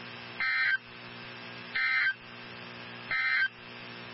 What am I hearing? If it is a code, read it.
Emergency Alert System (SAME) tones, heard as harsh buzzing bursts. It is heard from 0.3 to 3.6 s.
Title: NOAA Weather Radio 162.550